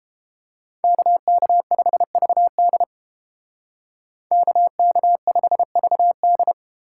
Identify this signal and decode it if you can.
Morse code, heard as a keyed tone, about 30 wpm, 700 Hz. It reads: KK5VD KK5VD